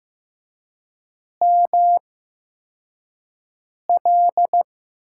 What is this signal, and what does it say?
Morse code, heard as a keyed tone, about 15 wpm, 700 Hz. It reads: M L
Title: USA Call Signs - 8x Character Spacing - 15wpm